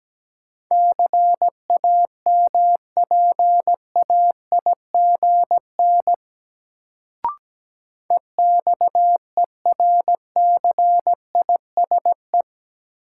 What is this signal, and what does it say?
Morse code, heard as a keyed tone, about 17 wpm, 700 Hz. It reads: CAMPAIGN EXERCISE
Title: Sentences from Top 1200 Words - New Word Drill Alt - 17wpm